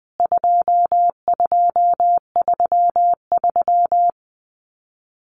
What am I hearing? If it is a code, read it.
Morse code, heard as a keyed tone, about 20 wpm, 700 Hz. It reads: 2233
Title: CWT Exchange 20wpm